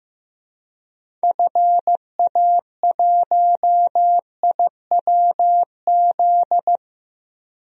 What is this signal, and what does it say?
Morse code, heard as a keyed tone, about 15 wpm, 700 Hz. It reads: FA1IWZ
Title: DXCC-100 Call Signs - 6 Characters - Repeated 3x - 15wpm